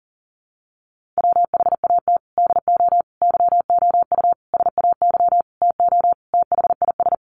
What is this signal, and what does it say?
Morse code, heard as a keyed tone, about 40 wpm, 700 Hz. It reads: WHAT DO YOU SAY TO THIS
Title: Sentences from Top 1400 Words 40wpm